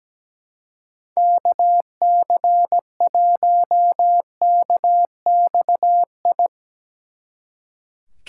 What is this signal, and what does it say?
Morse code, heard as a keyed tone, about 19 wpm, 700 Hz. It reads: KC1KXI
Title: Speed Racing - USA 2x3 Call Signs 17wpm